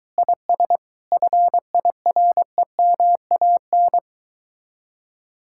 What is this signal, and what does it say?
Morse code, heard as a keyed tone, about 22 wpm, 700 Hz. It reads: IS FIREMAN